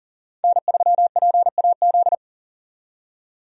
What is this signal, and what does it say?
Morse code, heard as a keyed tone, about 40 wpm, 700 Hz. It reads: N3PAZ
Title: USA Call Signs - 5 Characters - 40wpm